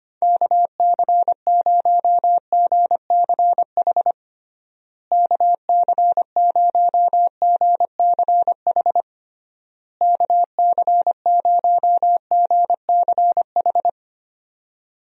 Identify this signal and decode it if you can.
Morse code, heard as a keyed tone, about 25 wpm, 700 Hz. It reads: KC0GCH KC0GCH KC0GCH
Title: USA Call Signs - 6 Characters - Repeated 3x - 25wpm